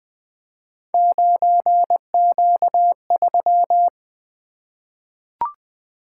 Morse code, heard as a keyed tone, about 20 wpm, 700 Hz. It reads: 9Q3
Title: Three Letter-Number Combinations - 20wpm